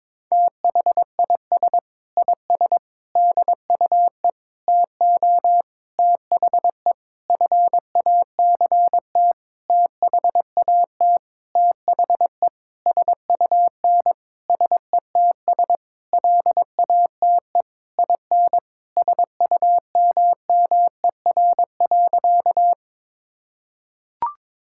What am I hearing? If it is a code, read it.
Morse code, heard as a keyed tone, about 22 wpm, 700 Hz. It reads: THIS IS DUE TO THE FACT THAT THE SUN SETS LATE IN SUMMER.
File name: practice / 22wpm / Sentences with 8-Word Phrases - 22wpm